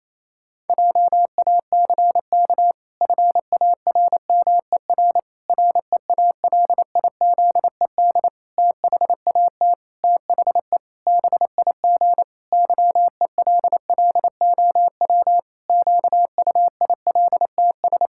Morse code, heard as a keyed tone, about 27 wpm, 700 Hz. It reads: JACK FARMER REALIZED THAT THE BIG YELLOW QUILTS
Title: Pangram Challenge v2 - 28wpm